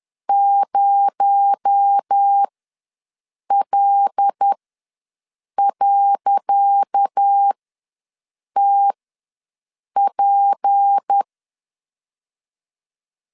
Morse code, heard as a keyed tone, about 10 wpm, spaced as for 6 wpm, 800 Hz. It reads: 0L.TP
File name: LEZIONI / 10 SIGNS / 10C DOT ws 10wpm 4213